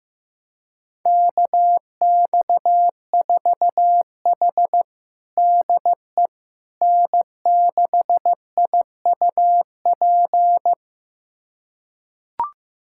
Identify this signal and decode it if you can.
Morse code, heard as a keyed tone, about 15 wpm, 700 Hz. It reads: KX4H DE N6IUP
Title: USA Callsign DE Callsign 15wpm